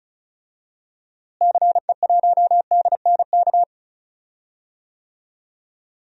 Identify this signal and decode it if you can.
Morse code, heard as a keyed tone, about 36 wpm, 700 Hz. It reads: CE1DNK
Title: DXCC-100 Call Signs - 6 Characters - Repeated 2x - 35wpm